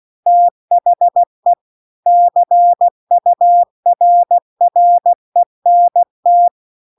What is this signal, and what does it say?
Morse code, heard as a keyed tone, about 16 wpm, 700 Hz. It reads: THE CURRENT